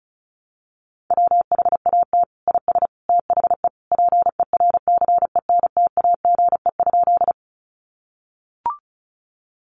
Morse code, heard as a keyed tone, about 34 wpm, 700 Hz. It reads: WHAT IS THE PERCENTAGE?
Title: Sentences from Top 1300 Words 35wpm